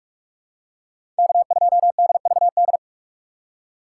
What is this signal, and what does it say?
Morse code, heard as a keyed tone, about 47 wpm, 700 Hz. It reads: K2DVD
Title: USA Call Signs - 5 Characters - 4x Character Spacing - 45wpm